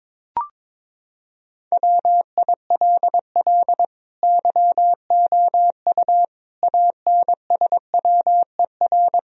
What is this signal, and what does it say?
Morse code, heard as a keyed tone, about 22 wpm, 700 Hz. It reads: WILL YOU ANSWER
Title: Sentences from Top 1000 Words - Review Part 4 - 22wpm